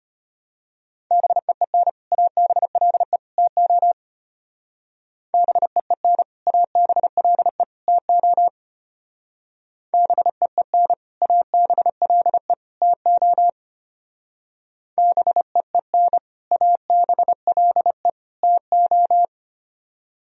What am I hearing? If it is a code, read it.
Morse code, heard as a keyed tone, about 31 wpm, 700 Hz. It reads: BEEN ABLE TO BEEN ABLE TO BEEN ABLE TO BEEN ABLE TO
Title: Speed-Racing - Sets of 3 Words 25wpm